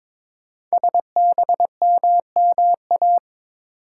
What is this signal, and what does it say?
Morse code, heard as a keyed tone, about 22 wpm, 700 Hz. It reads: SBMMA
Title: Code Groups - 5 Letters - 22wpm